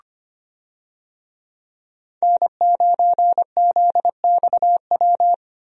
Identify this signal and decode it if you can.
Morse code, heard as a keyed tone, about 25 wpm, 700 Hz. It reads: N9ZXW
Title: USA Call Signs - 5 Characters - 25wpm